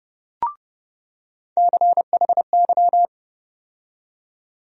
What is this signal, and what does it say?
Morse code, heard as a keyed tone, about 30 wpm, 700 Hz. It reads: CHY
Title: USA Call Sign Postfixes - 3 Letters 30wpm